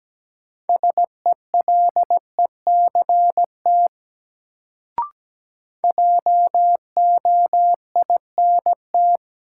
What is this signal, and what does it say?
Morse code, heard as a keyed tone, about 17 wpm, 700 Hz. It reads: SELECT JOINT
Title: Sentences from Top 1300 Words - New Word Drill - 17wpm